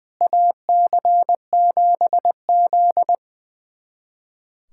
Morse code, heard as a keyed tone, about 20 wpm, 700 Hz. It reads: AC7Z
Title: USA Callsign DE Callsign - 4 Characters - 20wpm